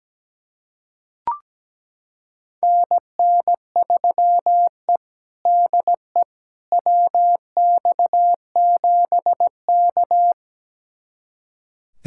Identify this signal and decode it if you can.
Morse code, heard as a keyed tone, about 17 wpm, 700 Hz. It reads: NN3E DE WX7K
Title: USA Callsign DE Callsign - 4 Characters - 17wpm